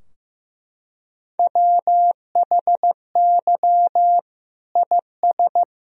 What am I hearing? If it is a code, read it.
Morse code, heard as a keyed tone, about 16 wpm, 700 Hz. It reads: WHY IS
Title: Sentences from Top 1000 Words - Review Part 2 - 15wpm